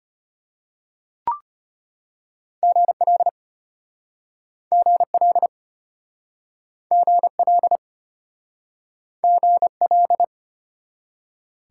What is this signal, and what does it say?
Morse code, heard as a keyed tone, about 34 wpm, 700 Hz. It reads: GL GL GL GL